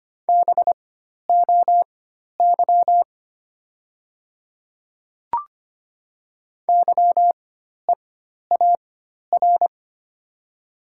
Morse code, heard as a keyed tone, about 25 wpm, spaced as for 12 wpm, 700 Hz. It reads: BOY YEAR
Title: 4x Character Spacing - Top 200 Words - 25wpm